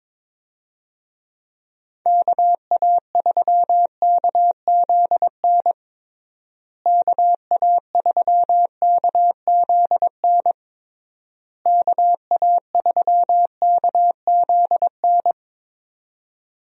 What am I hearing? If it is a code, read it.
Morse code, heard as a keyed tone, about 22 wpm, 700 Hz. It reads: KA3KZN KA3KZN KA3KZN
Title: USA Call Signs - 6 Characters - Repeated 3x - 22wpm